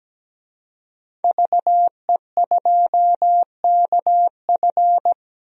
Morse code, heard as a keyed tone, about 17 wpm, 700 Hz. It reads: VE2KF
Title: DXCC-100 Call Signs - 5 Characters - 17wpm